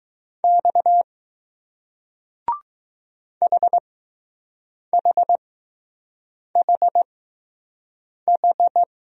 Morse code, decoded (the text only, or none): X HHHH